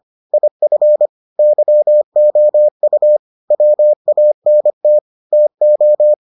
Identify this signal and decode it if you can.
Morse code, heard as a keyed tone, about 25 wpm, 600 Hz. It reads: IF YOU WANT TO